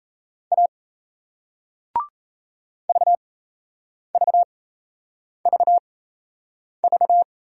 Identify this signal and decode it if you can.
Morse code, heard as a keyed tone, about 36 wpm, spaced as for 8 wpm, 700 Hz. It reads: A VVVV